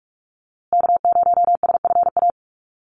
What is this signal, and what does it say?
Morse code, heard as a keyed tone, about 44 wpm, 690 Hz. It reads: K0SFA
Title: USA Call Signs Repeated 2x 45wpm